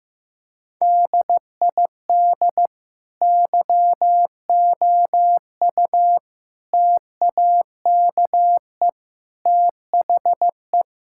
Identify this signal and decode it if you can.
Morse code, heard as a keyed tone, about 15 wpm, 700 Hz. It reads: DID YOU TAKE THE
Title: Sentences from Top 1000 Words - Review Part 6 - 15wpm